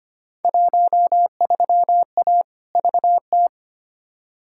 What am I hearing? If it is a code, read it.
Morse code, heard as a keyed tone, about 25 wpm, 700 Hz. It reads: 13A VT
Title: Field Day Exchanges Encore - 25wpm